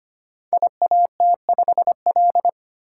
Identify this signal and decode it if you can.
Morse code, heard as a keyed tone, about 25 wpm, 700 Hz. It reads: IAT5L
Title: Code Groups - 5 Letter:Numbers - 25wpm